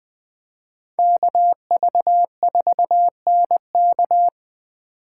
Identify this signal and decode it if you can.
Morse code, heard as a keyed tone, about 21 wpm, 700 Hz. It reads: KV4NK
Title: USA Call Signs - 2x Character Spacing - 20wpm